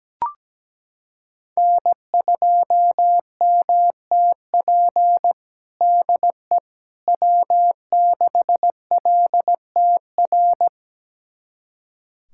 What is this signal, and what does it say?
Morse code, heard as a keyed tone, about 17 wpm, 700 Hz. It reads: N2MTP DE W6LTR
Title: USA Callsign DE Callsign - 5 Characters - 17wpm